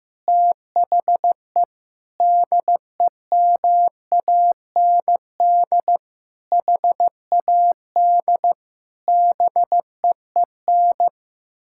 Morse code, heard as a keyed tone, about 15 wpm, 700 Hz. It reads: THE DEMAND HAD BEEN
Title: Sentences from Top 1000 Words - Review Part 6 - 15wpm